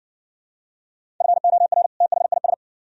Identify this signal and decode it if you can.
Morse code, heard as a keyed tone, about 60 wpm, 700 Hz. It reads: FOR THIS